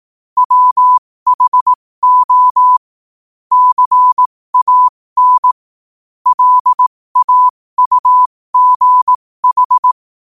Morse code, heard as a keyed tone, about 18 wpm, spaced as for 15 wpm, 1000 Hz. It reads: WHO CAN LAUGH
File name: qotdfiles / 15wpm